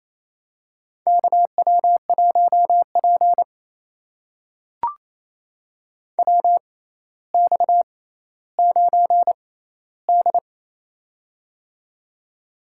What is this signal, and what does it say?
Morse code, heard as a keyed tone, about 28 wpm, spaced as for 14 wpm, 700 Hz. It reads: KW1P WX9D
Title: USA Call Signs - 4 Characters - 6x Character Spacing - 28wpm